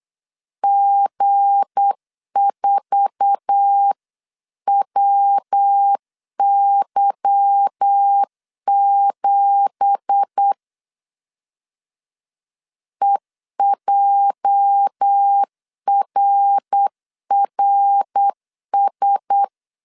Morse code, heard as a keyed tone, about 8 wpm, 800 Hz. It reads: G4WY7 EJRRS
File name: CW / DET / 1 9 1500C S C 8wpm 5653